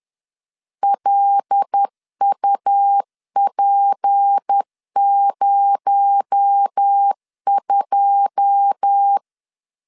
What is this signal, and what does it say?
Morse code, heard as a keyed tone, about 10 wpm, 800 Hz. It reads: LUP02
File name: CW / DET / 1 9 1500C S A 10wpm 4511